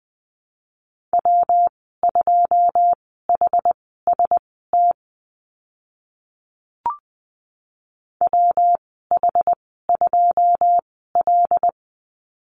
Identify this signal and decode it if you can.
Morse code, heard as a keyed tone, about 20 wpm, spaced as for 15 wpm, 700 Hz. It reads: W2HST WH2L
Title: 122 Lesson 20g - Number 3 - Call Signs 20wpm